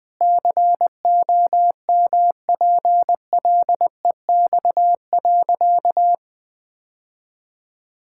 Morse code, heard as a keyed tone, about 21 wpm, 700 Hz. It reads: COMPLEX.
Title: Sentences with 4-Word Phrases - 20wpm